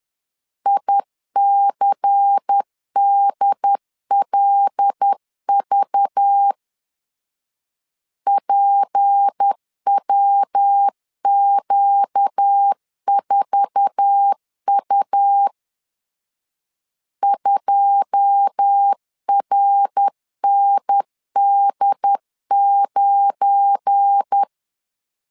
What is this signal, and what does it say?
Morse code, heard as a keyed tone, about 10 wpm, 800 Hz. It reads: ICDLV PWQ4U 2RND9
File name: CW / DET / 1 9 1500C S A 10wpm 4511